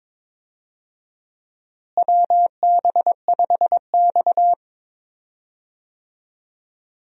Morse code, heard as a keyed tone, about 22 wpm, 700 Hz. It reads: WB5X
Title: USA Call Signs - 4 Characters - Repeated 3x - 22wpm